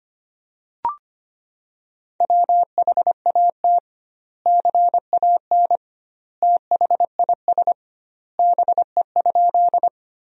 Morse code, heard as a keyed tone, about 25 wpm, 700 Hz. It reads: WHAT CAN THIS BE?
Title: 2x Word-Spacing - Sentences from Top 100 Words - 25wpm